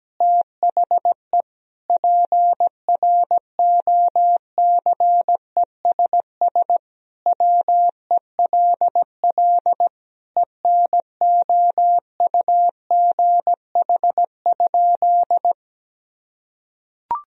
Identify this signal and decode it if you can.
Morse code, heard as a keyed tone, about 17 wpm, 700 Hz. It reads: THE PROCESS WELL ENOUGH?
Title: Sentences from Top 1000 Words - Review Part 3 - 17wpm